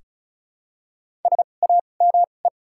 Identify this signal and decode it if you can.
Morse code, heard as a keyed tone, about 36 wpm, spaced as for 25 wpm, 700 Hz. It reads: SAME
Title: 2x Character Spacing - Top 200 Words - 35wpm